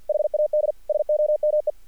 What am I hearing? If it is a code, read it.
Morse code, heard as a keyed tone, about 50 wpm, 600 Hz. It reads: HAD SOME
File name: archived / CRSnet